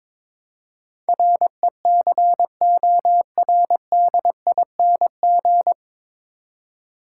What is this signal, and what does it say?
Morse code, heard as a keyed tone, about 22 wpm, 700 Hz. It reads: RECORDING
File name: 9 Letter Words - First 3 Letters Given - 22wpm